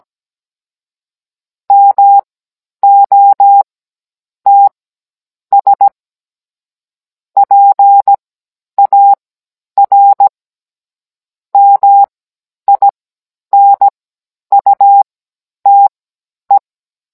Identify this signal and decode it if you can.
Morse code, heard as a keyed tone, about 17 wpm, spaced as for 9 wpm, 800 Hz. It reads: MOTS PAR MINUTE
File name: Cours-NR18-16mm